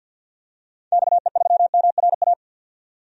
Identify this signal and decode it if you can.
Morse code, heard as a keyed tone, about 51 wpm, 700 Hz. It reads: XE3MRA
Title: DXCC-100 Call Signs - 6 Characters - 50wpm